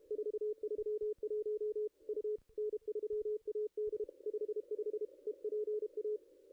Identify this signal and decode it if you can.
Morse code, heard as a keyed tone, about 28 wpm, 410 Hz, with radio noise. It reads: 431 U N3AD 55 EPA
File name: contests / ss-cw-k5tr-op-n5rz -0400